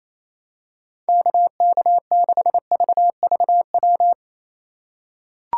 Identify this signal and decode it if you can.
Morse code, heard as a keyed tone, about 27 wpm, 700 Hz. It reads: KK6VVW
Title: USA Call Signs - 6 Characters - 2x Character Spacing - 28wpm